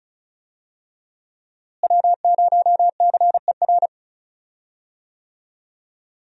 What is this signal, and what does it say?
Morse code, heard as a keyed tone, about 34 wpm, 700 Hz. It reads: W0CER
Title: USA Call Signs - 5 Characters - Repeated 3x - 35wpm